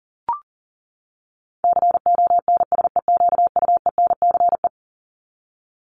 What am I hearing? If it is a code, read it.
Morse code, heard as a keyed tone, about 40 wpm, 700 Hz. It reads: CONSEQUENCE
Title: LY Suffix - Top 100 Words with Base Words 40wpm